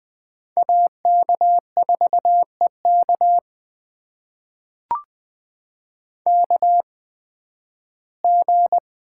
Morse code, heard as a keyed tone, about 20 wpm, spaced as for 13 wpm, 700 Hz. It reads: AK4EK KG
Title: USA Call Signs - 5 Characters - 8x Character Spacing - 20wpm